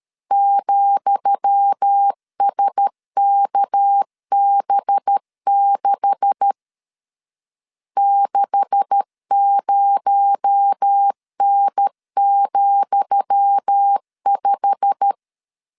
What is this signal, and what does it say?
Morse code, heard as a keyed tone, about 13 wpm, 800 Hz. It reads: ,SKB6 60N,5
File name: LEZIONI / 10 SIGNS / 10D COMA 12wpm 2544